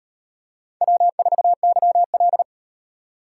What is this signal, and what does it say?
Morse code, heard as a keyed tone, about 37 wpm, 700 Hz. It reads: W4YL